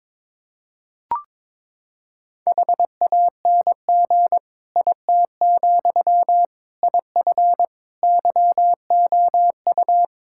Morse code, heard as a keyed tone, about 22 wpm, 700 Hz. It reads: HANG IT, IF YOU
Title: Sentences from Top 1600 Words 22wpm